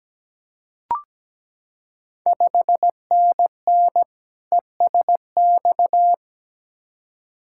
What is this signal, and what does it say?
Morse code, heard as a keyed tone, about 17 wpm, 700 Hz. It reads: 5NN ESX
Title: Ontario QSO Party Exchange 17wpm